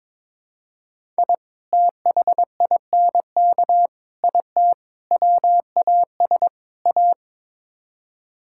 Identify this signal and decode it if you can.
Morse code, heard as a keyed tone, about 22 wpm, 700 Hz. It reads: I THINK IT WAS A
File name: practice / 22wpm / Build A Phrase - 5-Word Phrases - 22wpm